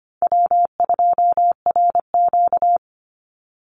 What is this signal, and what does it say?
Morse code, heard as a keyed tone, about 25 wpm, 700 Hz. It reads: W2RQ
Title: Easy Call Signs Repeated 2x - 25wpm